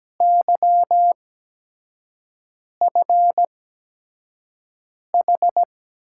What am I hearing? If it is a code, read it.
Morse code, heard as a keyed tone, about 17 wpm, spaced as for 5 wpm, 700 Hz. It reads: YFH
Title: USA Call Signs - 6 Characters - 8x Character Spacing - 17wpm